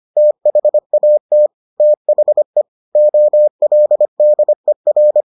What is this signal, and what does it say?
Morse code, heard as a keyed tone, about 25 wpm, 600 Hz. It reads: THAT THE OLDER